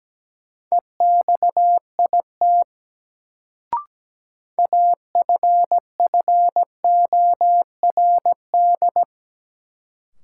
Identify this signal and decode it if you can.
Morse code, heard as a keyed tone, about 17 wpm, 700 Hz. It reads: EXIT AFFORD
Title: Sentences from Top 1100 Words - New Word Drill Alt - 17wpm